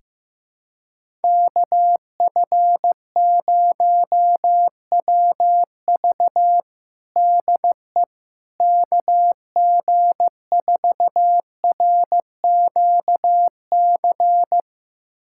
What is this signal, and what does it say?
Morse code, heard as a keyed tone, about 15 wpm, 700 Hz. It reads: KF0WV DE KG4RQC
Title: USA Callsign DE Callsign 15wpm